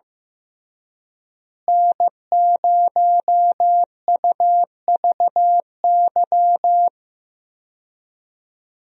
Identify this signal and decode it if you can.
Morse code, heard as a keyed tone, about 15 wpm, 700 Hz. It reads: N0UVY
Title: USA Call Signs - 5 Characters - Repeated 2x - 15wpm